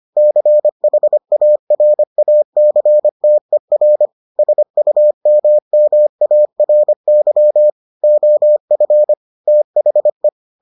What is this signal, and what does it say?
Morse code, heard as a keyed tone, about 25 wpm, 600 Hz. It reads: CHARACTER SUMMARY OF THE